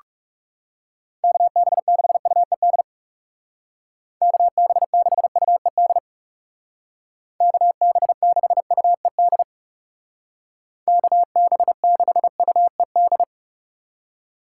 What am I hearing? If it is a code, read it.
Morse code, heard as a keyed tone, about 36 wpm, 700 Hz. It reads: KB6UED KB6UED KB6UED KB6UED
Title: Speed Racing - USA 2x3 Call Signs 30wpm